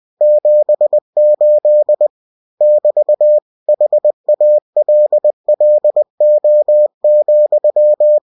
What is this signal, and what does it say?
Morse code, heard as a keyed tone, about 20 wpm, 600 Hz. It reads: 78 <BT> HALLO,